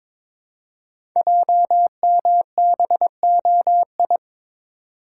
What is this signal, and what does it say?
Morse code, heard as a keyed tone, about 22 wpm, 700 Hz. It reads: JMBOI
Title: Code Groups - 5 Letters - 22wpm